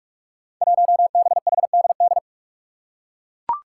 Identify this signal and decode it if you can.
Morse code, heard as a keyed tone, about 41 wpm, 700 Hz. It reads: 1BHDD
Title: Five Challenging Letter-Number Combinations 45wpm